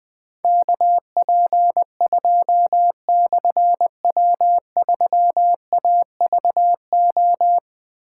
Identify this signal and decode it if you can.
Morse code, heard as a keyed tone, about 20 wpm, 700 Hz. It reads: KP2/W3AVO
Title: Difficult Call Signs 20wpm